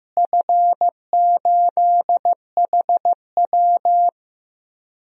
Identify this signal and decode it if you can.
Morse code, heard as a keyed tone, about 15 wpm, 700 Hz. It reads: F8HW
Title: DXCC-100 Call Signs - 4 Characters - Repeated 3x - 15wpm